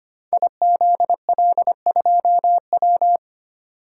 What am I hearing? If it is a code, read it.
Morse code, heard as a keyed tone, about 25 wpm, 700 Hz. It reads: IZL2W
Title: Code Groups - 5 Letter:Numbers - 25wpm